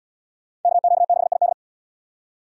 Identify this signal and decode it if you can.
Morse code, heard as a keyed tone, about 78 wpm, 700 Hz. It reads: KQ6ID